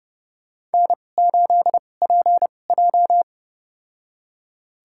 Morse code, heard as a keyed tone, about 30 wpm, spaced as for 22 wpm, 700 Hz. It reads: N8PJ
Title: USA Call Signs - 4 Characters - 2x Character Spacing - 30wpm